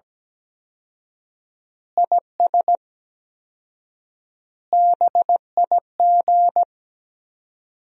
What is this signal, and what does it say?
Morse code, heard as a keyed tone, about 17 wpm, 700 Hz. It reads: IS BIG
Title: 4x Word-Spacing - Sentences from Top 400 Words - 17wpm